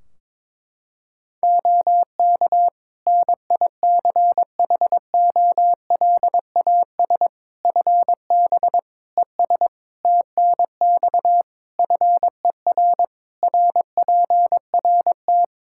Morse code, heard as a keyed tone, about 22 wpm, 700 Hz. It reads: OK NICHOLAS FB ES TNX FER RPRT
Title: Practice QSO - Exchange 2 - Info - 22wpm